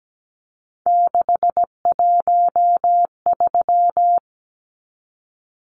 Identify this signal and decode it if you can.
Morse code, heard as a keyed tone, about 17 wpm, 700 Hz. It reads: 613
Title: Numbers - 3-Digit 17wpm